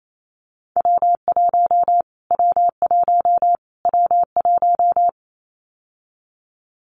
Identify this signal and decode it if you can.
Morse code, heard as a keyed tone, about 28 wpm, 700 Hz. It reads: W1 W1 W1
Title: Callsign Prefixes for Top 25 DXCC - Repeated 3x - 28wpm